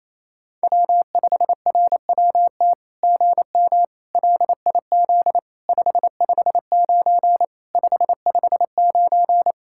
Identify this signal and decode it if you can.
Morse code, heard as a keyed tone, about 28 wpm, 700 Hz. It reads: W5RWT GM LIZ 559 559
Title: SOTA QSO Practice 28wpm